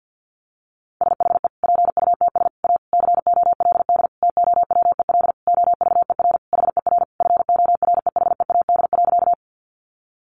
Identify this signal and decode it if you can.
Morse code, heard as a keyed tone, about 50 wpm, 700 Hz. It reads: SHE PUTS A COLD TOWEL OVER HER FOREHEAD.
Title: Easy Phrases 50wpm